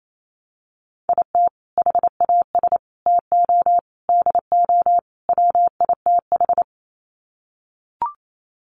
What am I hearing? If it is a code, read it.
Morse code, heard as a keyed tone, about 28 wpm, 700 Hz. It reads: IT HAS TO DO WITH
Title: Sets of 5 Words 28wpm